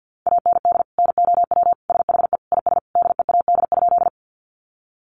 Morse code, heard as a keyed tone, about 50 wpm, 700 Hz. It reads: AND NOW SHE IS DEAD?
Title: Sentences from Top 1500 Words 50wpm